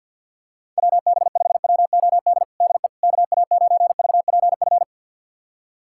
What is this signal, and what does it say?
Morse code, heard as a keyed tone, about 52 wpm, 700 Hz. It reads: WB5WOD DE KA9VPF
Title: USA Callsign DE Callsign - 6 Characters - 50wpm